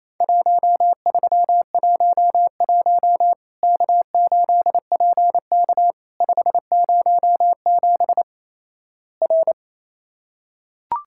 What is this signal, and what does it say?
Morse code, heard as a keyed tone, about 28 wpm, 700 Hz. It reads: 1311 K8PK 507 R
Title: Worked All Europe DX Contest - QTC Exchange - 28wpm